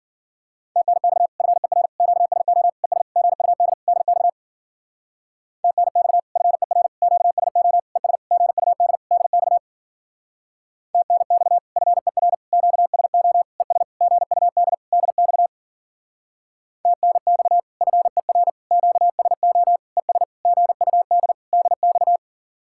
Morse code, heard as a keyed tone, about 50 wpm, 700 Hz. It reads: TNX FER QSO ES GUD DX TNX FER QSO ES GUD DX TNX FER QSO ES GUD DX TNX FER QSO ES GUD DX
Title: Speed-Racing Common QSO Phrases - 40wpm